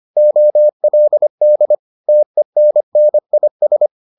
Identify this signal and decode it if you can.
Morse code, heard as a keyed tone, about 25 wpm, 600 Hz. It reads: OLD TENNIS